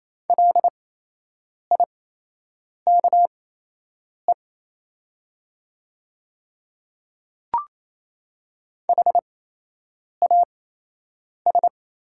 Morse code, heard as a keyed tone, about 28 wpm, spaced as for 8 wpm, 700 Hz. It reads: LIKE HAS
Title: 8x Character Spacing - Top 100 Words - 28wpm